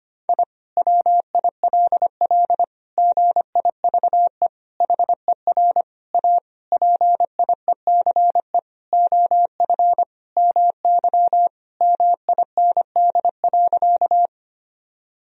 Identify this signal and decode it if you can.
Morse code, heard as a keyed tone, about 25 wpm, 700 Hz. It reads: I WILL GIVE HER A PIECE OF MY MIND.
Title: Sentences from Top 1000 Words - Review Part 2 - 25wpm